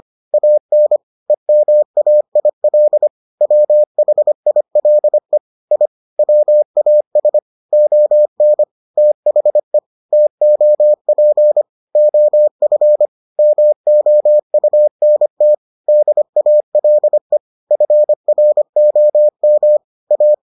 Morse code, heard as a keyed tone, about 25 wpm, 600 Hz. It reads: AN EMAIL WHILE I WAS ON THE TOP OF MOUNT DALE FROM A